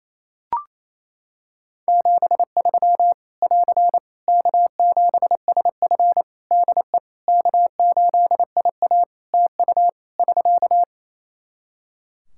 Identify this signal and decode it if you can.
Morse code, heard as a keyed tone, about 28 wpm, 700 Hz. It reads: 73 <AR> K7SF DE K8IA TU <SK>
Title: Practice QSO - Exchange 4 - Ending - 28wpm